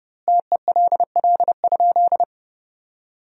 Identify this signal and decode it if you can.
Morse code, heard as a keyed tone, about 30 wpm, 700 Hz. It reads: TELL?
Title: 2x Word-Spacing - Sentences from Top 200 Words - 30wpm